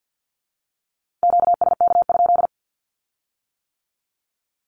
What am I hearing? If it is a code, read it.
Morse code, heard as a keyed tone, about 50 wpm, 700 Hz. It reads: QSK?
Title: Q-Codes with Spoken Meaning - 50wpm